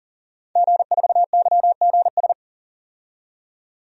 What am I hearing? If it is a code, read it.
Morse code, heard as a keyed tone, about 40 wpm, 700 Hz. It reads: G4YGS
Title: DXCC-100 Call Signs - 5 Characters - 40wpm